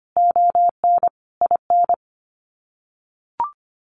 Morse code, heard as a keyed tone, about 25 wpm, 700 Hz. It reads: ON IN
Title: Sets of 5 Words Encore2 25wpm